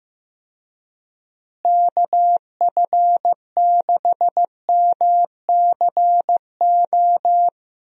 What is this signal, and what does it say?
Morse code, heard as a keyed tone, about 15 wpm, 700 Hz. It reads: KF6MCO
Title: USA Call Signs - 6 Characters - Repeated 3x - 15wpm